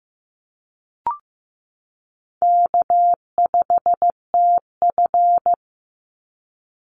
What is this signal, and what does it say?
Morse code, heard as a keyed tone, about 15 wpm, 700 Hz. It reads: K5TF
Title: Easy Call Signs - 15wpm